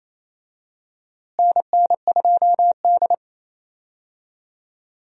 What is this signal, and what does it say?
Morse code, heard as a keyed tone, about 28 wpm, 700 Hz. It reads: NN2D
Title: USA Call Signs - 4 Characters - Repeated 3x - 28wpm